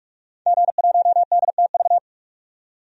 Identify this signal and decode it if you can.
Morse code, heard as a keyed tone, about 46 wpm, 700 Hz. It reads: G1DTV
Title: DXCC-100 Call Signs - 5 Characters - 45wpm